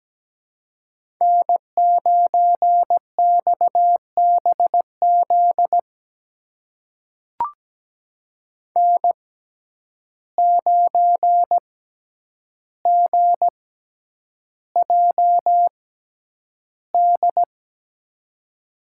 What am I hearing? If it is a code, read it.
Morse code, heard as a keyed tone, about 17 wpm, spaced as for 9 wpm, 700 Hz. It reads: N9XBZ N9GJD
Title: USA Call Signs - 5 Characters - 6x Character Spacing - 17wpm